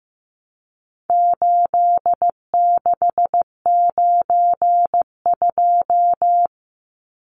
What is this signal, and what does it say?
Morse code, heard as a keyed tone, about 15 wpm, 700 Hz. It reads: 8692
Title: Numbers - 4-Digit 15wpm